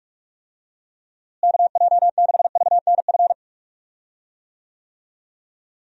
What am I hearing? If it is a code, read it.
Morse code, heard as a keyed tone, about 45 wpm, 700 Hz. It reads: KJ6VNF
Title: USA Call Signs - 6 Characters - Repeated 3x - 45wpm